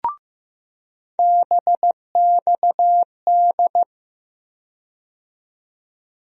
Morse code, heard as a keyed tone, about 15 wpm, 700 Hz. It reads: BXD
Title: USA Call Sign Postfixes - 3 Letters 15wpm